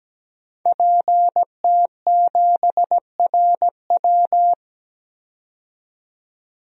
Morse code, heard as a keyed tone, about 17 wpm, 700 Hz. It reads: PT7RW
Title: DXCC-100 Call Signs - 5 Characters - 17wpm